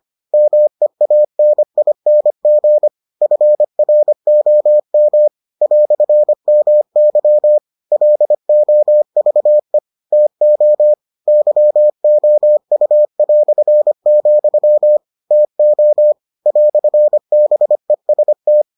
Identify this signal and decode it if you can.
Morse code, heard as a keyed tone, about 25 wpm, 600 Hz. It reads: MEANING FROM "MY LOVE TO YOU", TO "BEST